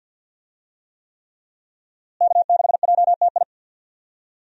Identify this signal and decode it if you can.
Morse code, heard as a keyed tone, about 51 wpm, 700 Hz. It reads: K6JTI
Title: USA Call Signs - 5 Characters - Repeated 3x - 50wpm